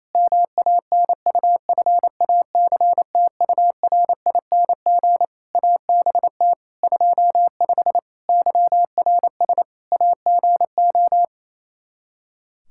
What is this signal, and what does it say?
Morse code, heard as a keyed tone, about 27 wpm, 700 Hz. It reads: MANUFACTURING ABT 25 YRS AGO
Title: K1USN SST QSO Practice 28wpm